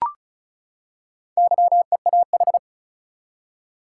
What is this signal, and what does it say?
Morse code, heard as a keyed tone, about 36 wpm, 700 Hz. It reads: YEAH
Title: Four Letter Words - 35wpm